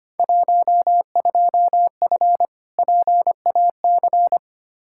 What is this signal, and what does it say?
Morse code, heard as a keyed tone, about 25 wpm, 700 Hz. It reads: 12F PAC
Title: Field Day Exchanges Encore - 25wpm